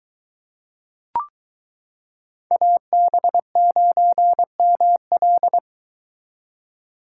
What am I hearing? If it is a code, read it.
Morse code, heard as a keyed tone, about 22 wpm, 700 Hz. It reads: AB9ML